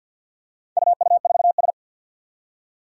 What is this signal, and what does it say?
Morse code, heard as a keyed tone, about 51 wpm, 700 Hz. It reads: UU4S
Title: Four Challenging Letter-Number Combinations 50wpm